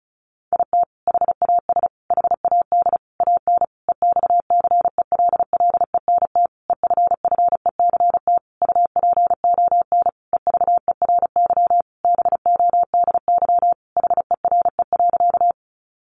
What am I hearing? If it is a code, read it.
Morse code, heard as a keyed tone, about 34 wpm, 700 Hz. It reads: IT HAS HAD AN EXCELLENT EFFECT UPON EVERY BODY HERE.
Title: Sentences from Top 1100 Words 35wpm